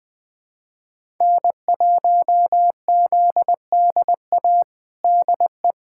Morse code, heard as a keyed tone, about 20 wpm, 700 Hz. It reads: N1ZDA DE
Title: USA Callsign DE Callsign - 5 Characters - 20wpm